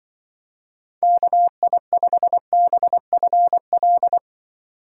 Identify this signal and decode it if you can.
Morse code, heard as a keyed tone, about 25 wpm, 700 Hz. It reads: KI5BFL